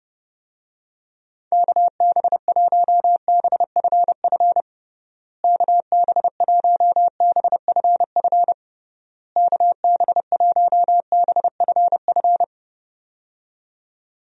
Morse code, heard as a keyed tone, about 30 wpm, 700 Hz. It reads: KB1BFF KB1BFF KB1BFF
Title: USA Call Signs - 6 Characters - Repeated 3x - 30wpm